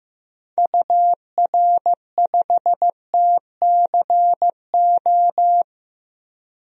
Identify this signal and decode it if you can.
Morse code, heard as a keyed tone, about 15 wpm, 700 Hz. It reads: UR5TCO
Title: DXCC-100 Call Signs - 6 Characters - Repeated 2x - 15wpm